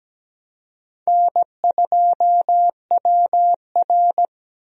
Morse code, heard as a keyed tone, about 17 wpm, 700 Hz. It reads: N2WR
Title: USA Call Signs - 4 Characters - Repeated 3x - 17wpm